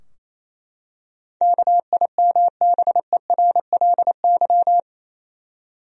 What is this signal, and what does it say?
Morse code, heard as a keyed tone, about 28 wpm, 700 Hz. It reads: KIMBERLY
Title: USA Common Names - 28wpm